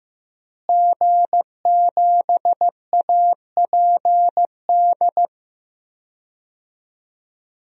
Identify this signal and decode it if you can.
Morse code, heard as a keyed tone, about 15 wpm, 700 Hz. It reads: G7APD
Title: DXCC-100 Call Signs - 5 Characters - Repeated 3x - 15wpm